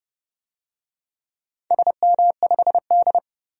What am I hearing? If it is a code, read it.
Morse code, heard as a keyed tone, about 28 wpm, 700 Hz. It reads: SM5D
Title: DXCC-100 Call Signs - 4 Characters - 30wpm